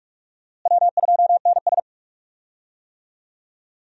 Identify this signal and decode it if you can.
Morse code, heard as a keyed tone, about 42 wpm, 700 Hz. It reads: W2NS
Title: USA Call Signs - 4 Characters - Repeated 3x - 45wpm